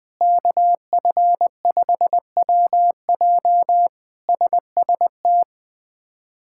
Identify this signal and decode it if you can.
Morse code, heard as a keyed tone, about 20 wpm, 700 Hz. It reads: KF5WJ SST
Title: K1USN SST QSO Practice - 20wpm